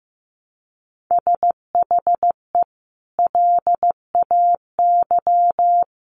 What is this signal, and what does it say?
Morse code, heard as a keyed tone, about 15 wpm, 700 Hz. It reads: SHE LAY
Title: Sentences from Top 900 Words 15wpm